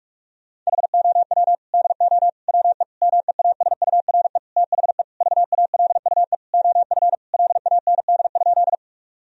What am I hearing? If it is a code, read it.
Morse code, heard as a keyed tone, about 42 wpm, 700 Hz. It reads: HOW DO WE MEASURE THE VALUE OF LAND?
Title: Sentences from Top 1000 Words - Review Part 3 - 45wpm